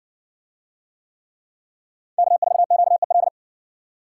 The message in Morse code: K4QEL

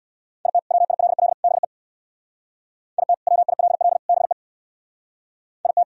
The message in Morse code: IT WILL BE IT WILL BE IT

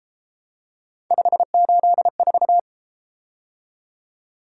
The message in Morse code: 584

